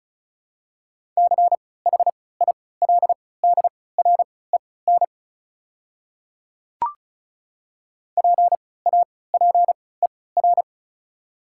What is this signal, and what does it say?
Morse code, heard as a keyed tone, about 36 wpm, spaced as for 20 wpm, 700 Hz. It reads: CHILDREN PAPER
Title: 3x Character Spacing - Top 300 Words - 35wpm